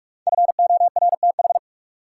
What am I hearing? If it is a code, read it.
Morse code, heard as a keyed tone, about 45 wpm, 700 Hz. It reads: FORTH
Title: Five Letter Words - 45wpm